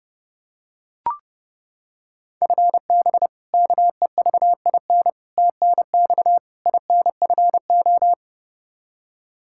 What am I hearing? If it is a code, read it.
Morse code, heard as a keyed tone, about 30 wpm, 700 Hz. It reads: FB KEVIN TNX INFO
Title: Practice QSO - Exchange 3 - More Info - 30wpm